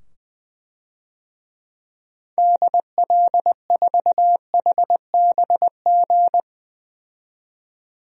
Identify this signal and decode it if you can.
Morse code, heard as a keyed tone, about 20 wpm, 700 Hz. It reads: DL4HBG
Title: DXCC-100 Call Signs - 6 Characters - 20wpm